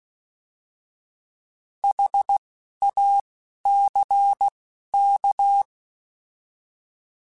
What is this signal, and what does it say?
Morse code, heard as a keyed tone, about 16 wpm, 800 Hz. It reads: H A C K